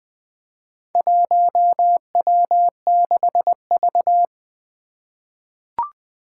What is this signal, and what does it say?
Morse code, heard as a keyed tone, about 20 wpm, 700 Hz. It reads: 1W6V
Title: Four Challenging Letter-Number Combinations 20wpm